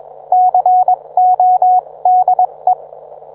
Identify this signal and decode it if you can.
Morse code, heard as a keyed tone, about 21 wpm, spaced as for 18 wpm, 730 Hz. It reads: CODE
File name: book